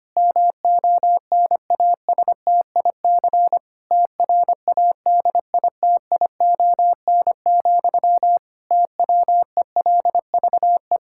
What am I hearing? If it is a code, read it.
Morse code, heard as a keyed tone, about 25 wpm, 700 Hz. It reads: MONASTIC TRADITION, TWELVE
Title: Pangram Challenge v2 - 25wpm